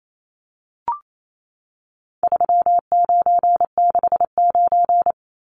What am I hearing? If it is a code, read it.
Morse code, heard as a keyed tone, about 28 wpm, 700 Hz. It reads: 3969
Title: Numbers - 4-Digit 28wpm